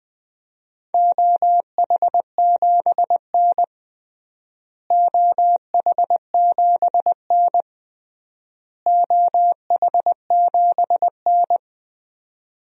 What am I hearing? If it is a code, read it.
Morse code, heard as a keyed tone, about 21 wpm, 700 Hz. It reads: OH7N OH7N OH7N
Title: DXCC-100 Call Signs - 4 Characters - Repeated 3x - 20wpm